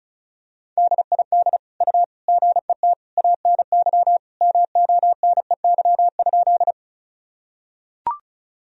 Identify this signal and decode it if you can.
Morse code, heard as a keyed tone, about 36 wpm, 700 Hz. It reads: DID U GET ANY MONEY?
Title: Sentences from Top 400 Words and Abbreviations 35wpm